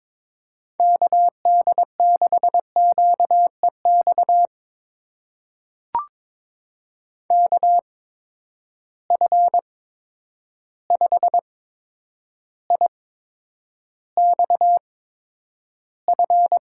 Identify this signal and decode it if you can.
Morse code, heard as a keyed tone, about 22 wpm, spaced as for 9 wpm, 700 Hz. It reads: KD6QEX KF5IXF
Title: USA Call Signs - 6 Characters - 8x Character Spacing - 22wpm